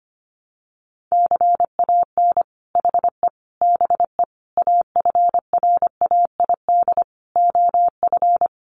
Morse code, heard as a keyed tone, about 25 wpm, 700 Hz. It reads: CAN HE BE AFRAID OF
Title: Sentences from Top 1500 Words 25wpm